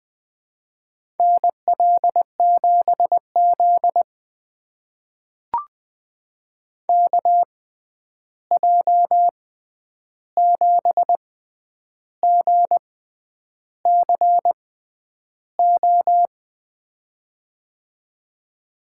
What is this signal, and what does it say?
Morse code, heard as a keyed tone, about 20 wpm, spaced as for 9 wpm, 700 Hz. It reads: NL7Z KJ7GCO
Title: USA Call Signs - 6x Character Spacing - 20wpm